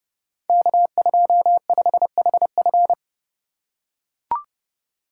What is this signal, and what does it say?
Morse code, heard as a keyed tone, about 30 wpm, 700 Hz. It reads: K25HF
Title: Code Groups - 5 Letter:Numbers - 30wpm